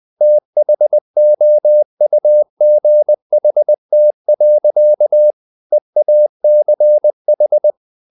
Morse code, heard as a keyed tone, about 20 wpm, 600 Hz. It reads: THOUGHT. EACH
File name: the old man and the sea / the old man and the sea part 1 20wpm